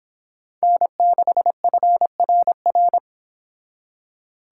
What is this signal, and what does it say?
Morse code, heard as a keyed tone, about 31 wpm, 700 Hz. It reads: N6FRR